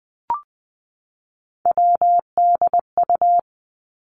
Morse code, heard as a keyed tone, about 19 wpm, 700 Hz. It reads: WDU
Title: Three Challenging Letter-Number Combinations 20wpm